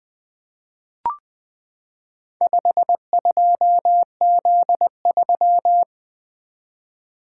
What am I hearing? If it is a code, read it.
Morse code, heard as a keyed tone, about 20 wpm, 700 Hz. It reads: 52Z3
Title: Four Letter-Number Combinations - 20wpm